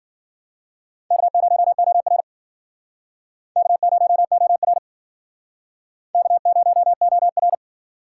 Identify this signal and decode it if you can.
Morse code, heard as a keyed tone, about 50 wpm, 700 Hz. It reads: K0OR K0OR K0OR